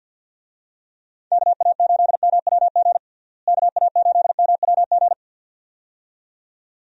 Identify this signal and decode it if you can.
Morse code, heard as a keyed tone, about 51 wpm, 700 Hz. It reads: KA8MWG KA8MWG
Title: USA Call Signs - 6 Characters - Repeated 2x - 50wpm